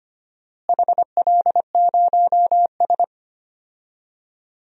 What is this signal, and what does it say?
Morse code, heard as a keyed tone, about 25 wpm, 700 Hz. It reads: HL0S
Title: DXCC-100 Call Signs - 4 Characters - Repeated 3x - 25wpm